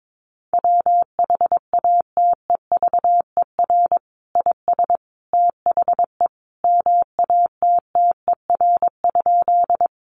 Morse code, heard as a keyed tone, about 22 wpm, 700 Hz. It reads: WHATEVER IS THE MATTER?
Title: Sentences from Top 1100 Words 22wpm